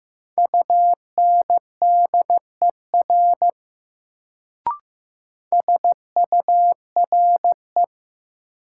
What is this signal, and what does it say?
Morse code, heard as a keyed tone, about 15 wpm, 700 Hz. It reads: UNDER SURE
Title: Sentences from Top 300 Words - New Word Drill - 15wpm